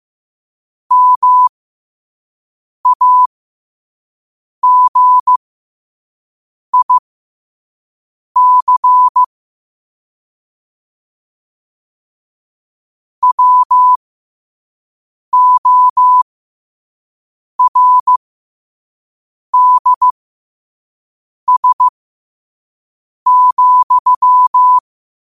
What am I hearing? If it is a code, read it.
Morse code, heard as a keyed tone, about 15 wpm, spaced as for 5 wpm, 1000 Hz. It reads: MAGIC WORDS,
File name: qotdfiles / 5wpm